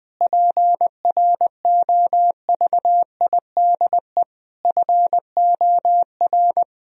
Morse code, heard as a keyed tone, about 20 wpm, 700 Hz. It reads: PROVIDE FOR
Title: Sentences from Top 1000 Words - Review Part 3 - 20wpm